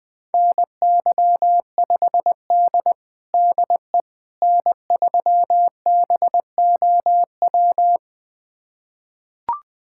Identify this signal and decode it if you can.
Morse code, heard as a keyed tone, about 20 wpm, 700 Hz. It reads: NY5D DE N3BOW
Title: USA Callsign DE Callsign 20wpm